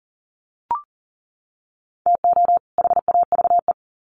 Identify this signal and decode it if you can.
Morse code, heard as a keyed tone, about 37 wpm, 700 Hz. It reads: TO HAVE
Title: Sets of 2 Words 40wpm